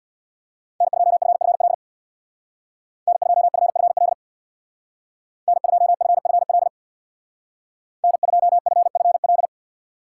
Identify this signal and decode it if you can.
Morse code, heard as a keyed tone, about 63 wpm, 700 Hz. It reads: N2FFL N2FFL N2FFL N2FFL